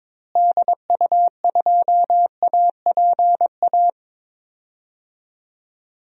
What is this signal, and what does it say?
Morse code, heard as a keyed tone, about 21 wpm, 700 Hz. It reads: DU2APA
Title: DXCC-100 Call Signs - 6 Characters - Repeated 3x - 22wpm